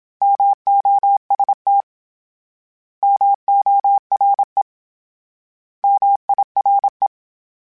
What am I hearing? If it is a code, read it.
Morse code, heard as a keyed tone, about 27 wpm, 800 Hz. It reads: MOST MORE MIRE
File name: Misc / Misc-mp3s / Misc @25